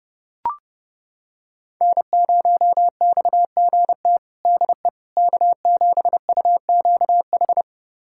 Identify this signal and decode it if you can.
Morse code, heard as a keyed tone, about 32 wpm, 700 Hz. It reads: N0XGT DE K7UQH
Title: USA Callsign DE Callsign - 5 Characters - 30wpm